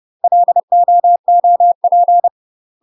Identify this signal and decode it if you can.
Morse code, heard as a keyed tone, about 30 wpm, 700 Hz. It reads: LOOP
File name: UK MASTER QSOs TRAINING / 30WPM / UK MASTER QSOs 0000